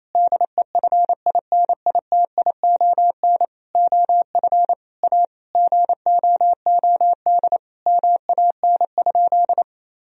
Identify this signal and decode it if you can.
Morse code, heard as a keyed tone, about 28 wpm, 700 Hz. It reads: DEFINITION OF A GOOD MAN?
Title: Sentences from Top 1000 Words - Review Part 4 - 28wpm